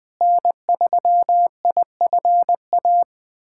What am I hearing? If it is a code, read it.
Morse code, heard as a keyed tone, about 20 wpm, 700 Hz. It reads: N3IFA
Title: USA Callsign DE Callsign - 5 Characters - 20wpm